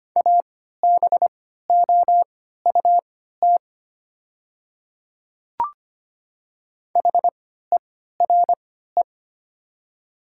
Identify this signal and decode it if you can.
Morse code, heard as a keyed tone, about 25 wpm, spaced as for 14 wpm, 690 Hz. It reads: ABOUT HERE
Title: 3x Character Spacing - Top 500 Words - 25wpm